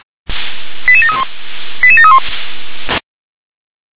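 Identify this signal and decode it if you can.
Five-tone selective call (ZVEI), heard as quick runs of five tones, tones of about 80 ms. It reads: 88731 88731
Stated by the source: ZVEI five-tone call received on an SDR.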